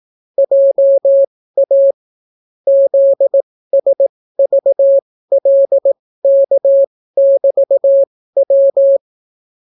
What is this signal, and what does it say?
Morse code, heard as a keyed tone, about 18 wpm, spaced as for 14 wpm, 550 Hz. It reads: JA ZSVLK<BT>W